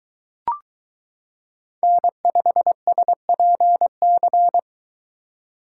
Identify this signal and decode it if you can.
Morse code, heard as a keyed tone, about 23 wpm, 700 Hz. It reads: N5SPC